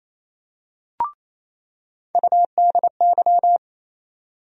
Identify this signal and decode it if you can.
Morse code, heard as a keyed tone, about 28 wpm, 700 Hz. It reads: UDY
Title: USA Call Sign Postfixes - 3 Letters 28wpm